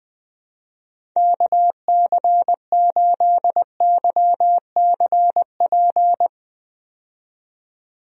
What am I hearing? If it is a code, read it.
Morse code, heard as a keyed tone, about 20 wpm, 700 Hz. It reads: KC8YCP
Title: USA Call Signs - 6 Characters - 20wpm